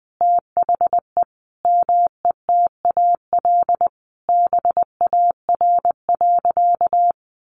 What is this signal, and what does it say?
Morse code, heard as a keyed tone, about 20 wpm, 700 Hz. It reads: THE METAL BAR.
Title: Additional Easy Phrases 20wpm